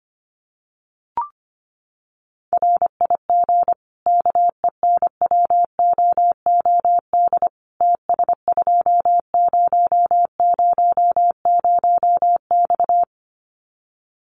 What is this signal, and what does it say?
Morse code, heard as a keyed tone, about 25 wpm, 700 Hz. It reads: RIG KENWOOD TS2000X
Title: Common QSO Phrases 25wpm